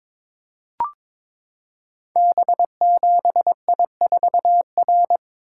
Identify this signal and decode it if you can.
Morse code, heard as a keyed tone, about 21 wpm, 700 Hz. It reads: B7I4R
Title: Code Groups - 5 Letter:Numbers - 22wpm